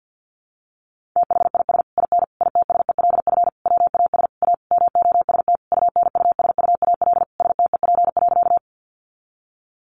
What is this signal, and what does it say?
Morse code, heard as a keyed tone, about 50 wpm, 700 Hz. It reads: THIS IN ITSELF WAS A MOST UNUSUAL STEP.
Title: Sentences from Top 1400 Words 50wpm